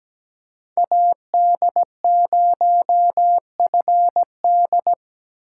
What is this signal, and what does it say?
Morse code, heard as a keyed tone, about 17 wpm, 700 Hz. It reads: AD0FD
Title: USA Call Signs - 5 Characters - 3x Character Spacing - 17wpm